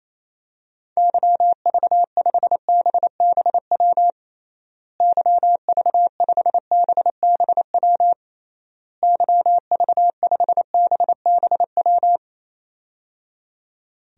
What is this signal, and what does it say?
Morse code, heard as a keyed tone, about 28 wpm, 700 Hz. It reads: YV5BBW YV5BBW YV5BBW
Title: DXCC-100 Call Signs - 6 Characters - Repeated 3x - 28wpm